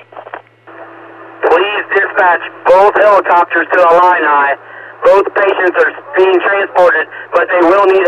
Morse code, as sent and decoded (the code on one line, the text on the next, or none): none